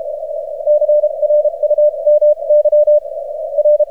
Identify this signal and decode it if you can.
Morse code, heard as a keyed tone, about 35 wpm, spaced as for 29 wpm, 600 Hz. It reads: CRU MY R
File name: archived / CRSnet